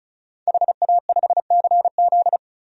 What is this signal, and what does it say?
Morse code, heard as a keyed tone, about 34 wpm, 700 Hz. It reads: HA5CZ
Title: DXCC-100 Call Signs - 5 Characters - 35wpm